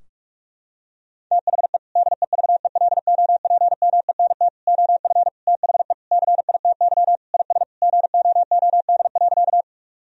Morse code, heard as a keyed tone, about 46 wpm, 700 Hz. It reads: THE DEVELOPMENT OF THE CITY IS GOOD.
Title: Sentences from Top 1000 Words - Review Part 4 - 45wpm